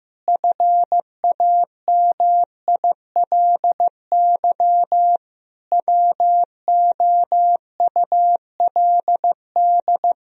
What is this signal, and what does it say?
Morse code, heard as a keyed tone, about 15 wpm, 700 Hz. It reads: FAMILY WOULD
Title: Sentences from Top 1000 Words - Review Part 3 - 15wpm